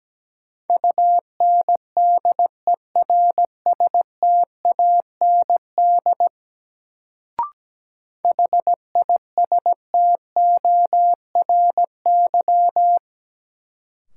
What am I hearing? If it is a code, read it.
Morse code, heard as a keyed tone, about 17 wpm, 700 Hz. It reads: UNDERSTAND HISTORY
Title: Sentences from Top 600 Words - New Word Drill - 17wpm